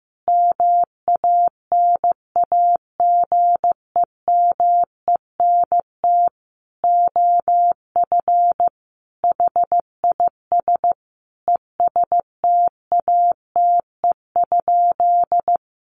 Morse code, heard as a keyed tone, about 15 wpm, 700 Hz. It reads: MANAGEMENT OF HIS ESTATE?
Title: Sentences from Top 1200 Words 15wpm